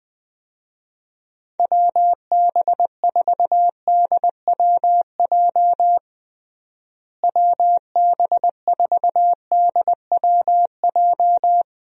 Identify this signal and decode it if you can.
Morse code, heard as a keyed tone, about 19 wpm, 700 Hz. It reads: WB4DWJ WB4DWJ
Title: USA Call Signs - 6 Characters - Repeated 2x - 20wpm